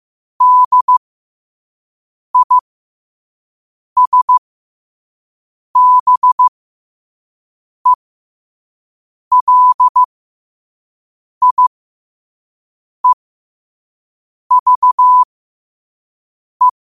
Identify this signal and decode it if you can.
Morse code, heard as a keyed tone, about 15 wpm, spaced as for 5 wpm, 1000 Hz. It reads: DISBELIEVE
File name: qotdfiles / 5wpm